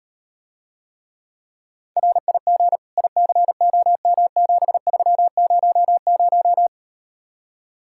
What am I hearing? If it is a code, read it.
Morse code, heard as a keyed tone, about 38 wpm, 700 Hz. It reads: RIG ICOM7300